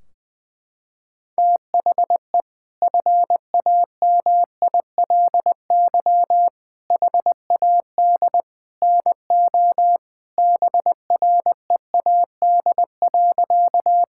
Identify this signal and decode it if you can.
Morse code, heard as a keyed tone, about 20 wpm, 700 Hz. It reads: THE FAMILY HAD NO BREAD.
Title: Sentences from Top 1600 Words 20wpm